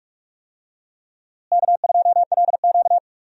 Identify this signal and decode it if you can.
Morse code, heard as a keyed tone, about 44 wpm, 700 Hz. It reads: K2LQ
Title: USA Call Signs - 4 Characters - Repeated 3x - 45wpm